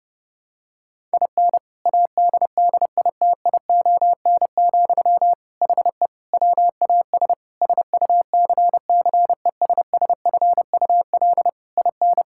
Morse code, heard as a keyed tone, about 32 wpm, 700 Hz. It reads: IN ADDITION, HE WAS SUCCESSFUL IN
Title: Sentences from Top 1000 Words - Review Part 6 - 30wpm